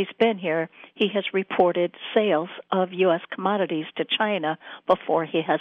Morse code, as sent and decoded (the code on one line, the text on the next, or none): none